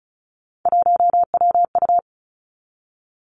Morse code, heard as a keyed tone, about 36 wpm, 700 Hz. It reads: 1WU
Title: Three Challenging Letter-Number Combinations 35wpm